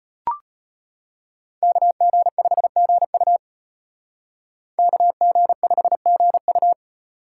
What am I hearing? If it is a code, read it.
Morse code, heard as a keyed tone, about 33 wpm, 700 Hz. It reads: KG5GU KG5GU